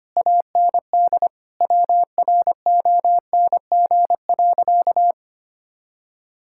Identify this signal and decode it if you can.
Morse code, heard as a keyed tone, about 25 wpm, 700 Hz. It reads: AND WRONG.
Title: Sentences from Top 1600 Words 25wpm